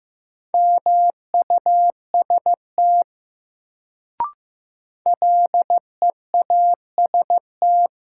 Morse code, heard as a keyed tone, about 15 wpm, 700 Hz. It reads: MUST LEAST
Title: Sets of 2 Words - New Word Drill - 15wpm